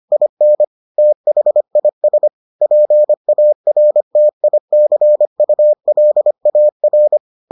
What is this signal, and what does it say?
Morse code, heard as a keyed tone, about 25 wpm, 600 Hz. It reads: IN THIS PARTICULAR